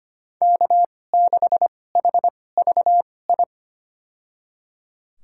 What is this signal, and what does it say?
Morse code, heard as a keyed tone, about 25 wpm, spaced as for 18 wpm, 700 Hz. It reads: K6HVI
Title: USA Call Signs - 5 Characters - 2x Character Spacing - 25wpm